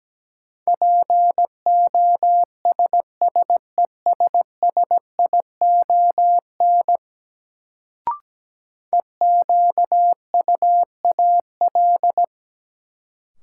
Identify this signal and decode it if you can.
Morse code, heard as a keyed tone, about 17 wpm, 700 Hz. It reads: POSSESSION EQUAL
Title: Sentences from Top 1400 Words - New Word Drill Alt - 17wpm